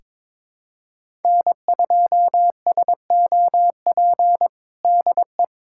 Morse code, heard as a keyed tone, about 22 wpm, 700 Hz. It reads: N2SOP DE
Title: USA Callsign DE Callsign - 5 Characters - 22wpm